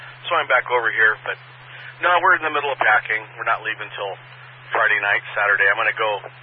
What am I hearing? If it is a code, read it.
Emergency Alert System (SAME) tones (header bursts), heard as harsh buzzing bursts, with speech. It is heard from 0.8 to 5.3 s.